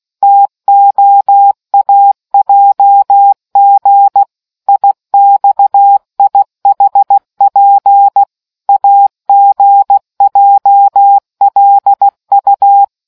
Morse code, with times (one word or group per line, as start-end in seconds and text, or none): 0.2-4.2 TOAJG
4.7-8.2 IXIHP
8.7-12.8 AGJLU